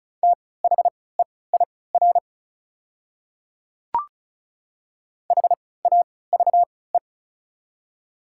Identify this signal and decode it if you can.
Morse code, heard as a keyed tone, about 34 wpm, spaced as for 20 wpm, 700 Hz. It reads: THEIR HAVE
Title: 3x Character Spacing - Top 100 Words - 35wpm